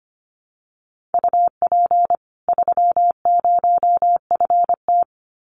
Morse code, heard as a keyed tone, about 25 wpm, 700 Hz. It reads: UP 30FT
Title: Common QSO Phrases 25wpm